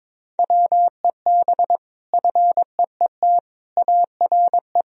Morse code, heard as a keyed tone, about 22 wpm, 700 Hz. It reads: WEB FEET ARE
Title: Sentences from Top 1000 Words - Review Part 4 - 22wpm